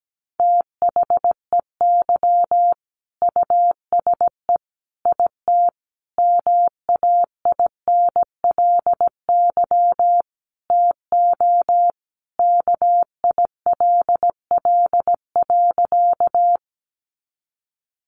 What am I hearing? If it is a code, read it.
Morse code, heard as a keyed tone, about 17 wpm, 700 Hz. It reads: THEY USE IT MAINLY TO KILL.
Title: Sentences from Top 1400 Words 17wpm